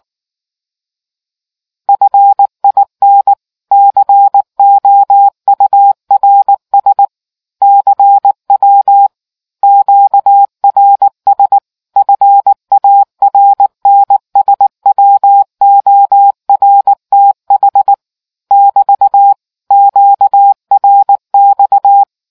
Morse code, heard as a keyed tone, about 19 wpm, 800 Hz. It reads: FIN COURS CW QRS FARNSWORTH <BT> QRX